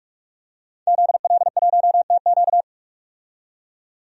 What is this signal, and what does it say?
Morse code, heard as a keyed tone, about 42 wpm, 700 Hz. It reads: ZL1TQ
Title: DXCC-100 Call Signs - 5 Characters - 45wpm